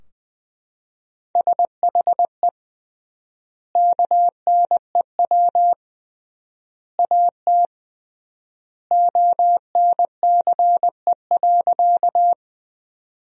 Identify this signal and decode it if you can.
Morse code, heard as a keyed tone, about 21 wpm, 700 Hz. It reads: SHE KNEW AT ONCE.
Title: 3x Word-Spacing - Sentences from Top 400 Words - 20wpm